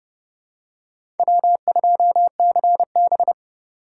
Morse code, heard as a keyed tone, about 30 wpm, 700 Hz. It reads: W2CB
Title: USA Call Signs - 4 Characters - 2x Character Spacing - 30wpm